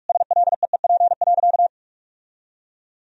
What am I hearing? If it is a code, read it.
Morse code, heard as a keyed tone, about 46 wpm, 700 Hz. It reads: SLEEP.
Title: Sentences from Top 1000 Words - Review Part 2 - 45wpm